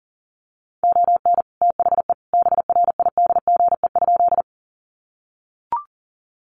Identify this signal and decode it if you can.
Morse code, heard as a keyed tone, about 40 wpm, 700 Hz. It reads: ON THE BRIDGE?
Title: Sentences from Top 1200 Words 40wpm